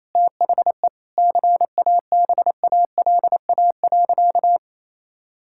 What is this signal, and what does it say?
Morse code, heard as a keyed tone, about 28 wpm, 700 Hz. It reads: THE CABALA.
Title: Pangram Challenge v2 - 28wpm